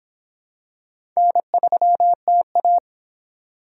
Morse code, heard as a keyed tone, about 26 wpm, 700 Hz. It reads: N3TA